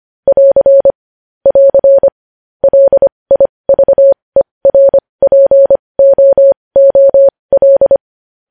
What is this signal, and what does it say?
Morse code, heard as a keyed tone, about 25 wpm, spaced as for 20 wpm, 550 Hz. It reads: <AR> <AR> LIVERPOOL
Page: News Headlines in Morse Code at 20 WPM